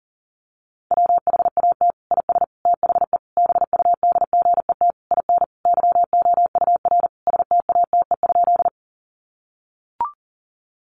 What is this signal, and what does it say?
Morse code, heard as a keyed tone, about 40 wpm, 700 Hz. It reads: WHAT IS THE BUDGET IN YOUR STATE?
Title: Sentences from Top 1200 Words 40wpm